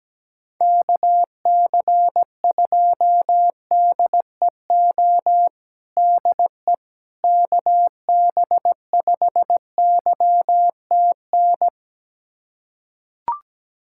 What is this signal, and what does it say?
Morse code, heard as a keyed tone, about 17 wpm, 700 Hz. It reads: KC2DEO DE KB5YTN
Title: USA Callsign DE Callsign 17wpm